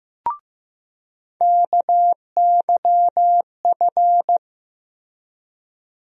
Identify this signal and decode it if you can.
Morse code, heard as a keyed tone, about 15 wpm, 700 Hz. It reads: KYF
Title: USA Call Sign Postfixes - 3 Letters 15wpm